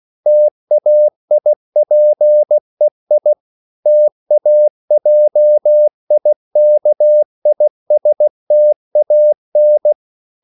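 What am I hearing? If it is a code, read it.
Morse code, heard as a keyed tone, about 16 wpm, 600 Hz. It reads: TAIPEI TAJIKISTAN
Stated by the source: CW practice of national capitals at 16 words per minute, from ON6ZQ.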